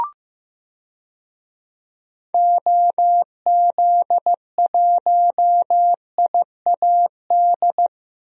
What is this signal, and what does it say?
Morse code, heard as a keyed tone, about 15 wpm, 700 Hz. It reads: OZ1IAD
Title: DXCC-100 Call Signs - 6 Characters - Repeated 2x - 15wpm